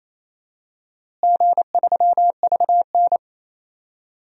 Morse code, heard as a keyed tone, about 28 wpm, 700 Hz. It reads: G3VN
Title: DXCC-100 Call Signs - 4 Characters - Repeated 2x - 28wpm